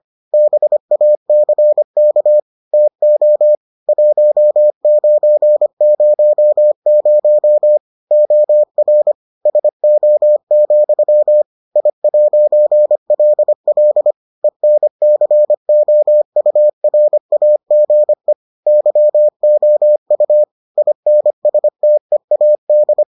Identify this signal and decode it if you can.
Morse code, heard as a keyed tone, about 25 wpm, 600 Hz. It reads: BACK TO 1900 OR SO, I'LL ENCOURAGE YOU INSTEAD